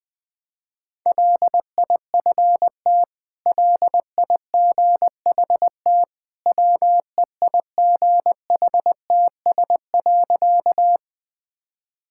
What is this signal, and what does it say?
Morse code, heard as a keyed tone, about 21 wpm, 700 Hz. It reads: LIFT LIGHT WEIGHTS.
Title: Easy Sentences with Hints - 20wpm